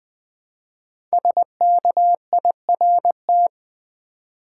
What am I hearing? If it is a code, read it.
Morse code, heard as a keyed tone, about 20 wpm, 700 Hz. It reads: SKIRT
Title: 5 Letter Words - First 3 Letters Given - 20wpm